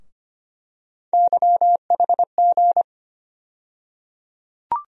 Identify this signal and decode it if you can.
Morse code, heard as a keyed tone, about 25 wpm, 700 Hz. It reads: YHG
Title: USA Call Sign Postfixes - 3 Letters 25wpm